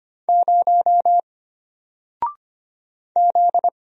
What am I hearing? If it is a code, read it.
Morse code, heard as a keyed tone, about 25 wpm, 700 Hz. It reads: 0 Z
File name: Single-Letter-Number-25wpm